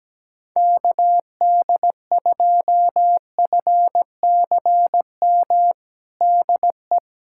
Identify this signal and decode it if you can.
Morse code, heard as a keyed tone, about 17 wpm, 700 Hz. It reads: KD2FCM DE
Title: USA Callsign DE Callsign - 6 Characters - 17wpm